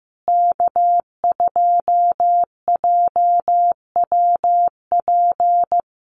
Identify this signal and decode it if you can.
Morse code, heard as a keyed tone, about 15 wpm, 700 Hz. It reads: K2JWP
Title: USA Call Signs Repeated 3x 15wpm